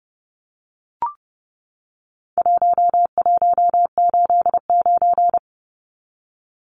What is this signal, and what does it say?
Morse code, heard as a keyed tone, about 30 wpm, 700 Hz. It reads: 1189
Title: Numbers - 4-Digit 30wpm